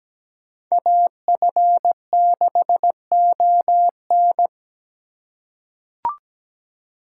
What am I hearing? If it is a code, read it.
Morse code, heard as a keyed tone, about 17 wpm, 700 Hz. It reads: AF6ON
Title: USA Call Signs - 5 Characters - 2x Character Spacing - 17wpm